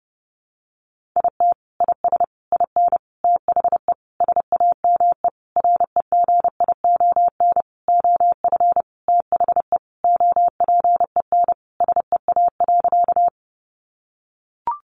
Morse code, heard as a keyed tone, about 30 wpm, 700 Hz. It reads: IT IS IN THE SAME REGION OF THE OPEN SEA.
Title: Sentences from Top 1500 Words 30wpm